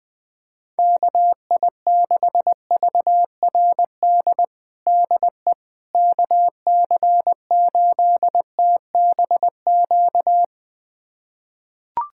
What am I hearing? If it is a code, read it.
Morse code, heard as a keyed tone, about 20 wpm, 700 Hz. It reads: KI6VRD DE KC8TBQ
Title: USA Callsign DE Callsign 20wpm